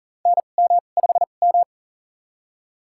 Morse code, heard as a keyed tone, about 38 wpm, spaced as for 28 wpm, 700 Hz. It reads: NM5M
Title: USA Call Signs - 4 Characters - 2x Character Spacing - 40wpm